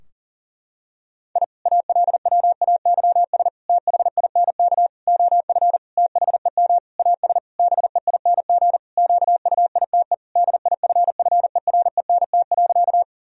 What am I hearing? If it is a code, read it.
Morse code, heard as a keyed tone, about 40 wpm, 700 Hz. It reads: I ALWAYS THINK OF THEM AS BEING QUITE DIFFERENT.
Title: Sentences from Top 1000 Words - Review Part 3 - 40wpm